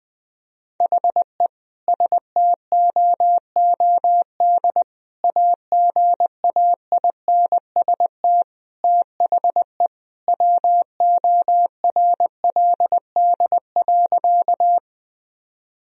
Morse code, heard as a keyed tone, about 21 wpm, 700 Hz. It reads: HE STOOD AGAINST THE WORLD.
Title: Sentences from Top 1000 Words - Review Part 2 - 20wpm